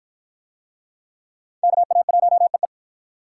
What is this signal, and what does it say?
Morse code, heard as a keyed tone, about 53 wpm, 700 Hz. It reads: KA1EE